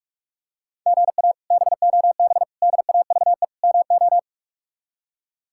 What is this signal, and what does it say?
Morse code, heard as a keyed tone, about 41 wpm, 700 Hz. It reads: GA BOB DAVE MO
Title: K1USN SST QSO Practice - 45wpm